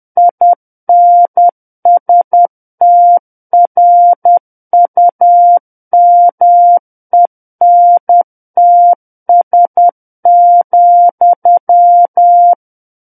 Morse code, heard as a keyed tone, about 10 wpm, 700 Hz. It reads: INSTRUMENTS,
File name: Book One / 01-Chapter 1 - 10 WPM